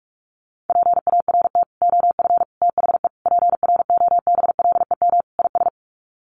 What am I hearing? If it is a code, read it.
Morse code, heard as a keyed tone, about 44 wpm, 690 Hz. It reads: PART OF THE PROBLEM IS
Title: Sets of 5 Words Encore2 45wpm